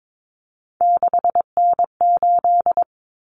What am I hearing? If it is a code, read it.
Morse code, heard as a keyed tone, about 22 wpm, 700 Hz. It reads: 6N8
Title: Common Cut Numbers - 3-Digit 22wpm